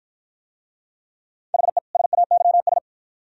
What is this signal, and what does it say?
Morse code, heard as a keyed tone, about 46 wpm, 700 Hz. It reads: HE SAYS